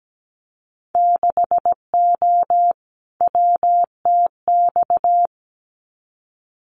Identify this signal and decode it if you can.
Morse code, heard as a keyed tone, about 17 wpm, 700 Hz. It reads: 6O WTX
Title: Winter Field Day Exchanges Encore 17wpm